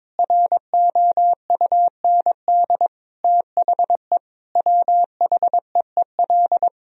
Morse code, heard as a keyed tone, about 22 wpm, 700 Hz. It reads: ROUND THE WHEEL
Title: Sentences from Top 1000 Words - Review Part 3 - 22wpm